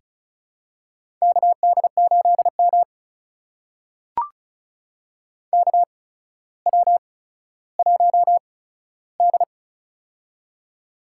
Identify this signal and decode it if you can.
Morse code, heard as a keyed tone, about 34 wpm, spaced as for 15 wpm, 700 Hz. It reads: KD8M KW1D
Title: USA Call Signs - 4 Characters - 8x Character Spacing - 35wpm